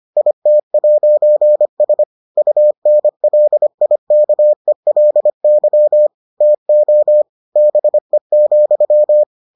Morse code, heard as a keyed tone, about 25 wpm, 600 Hz. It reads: IT'S UNLIKELY TO BE,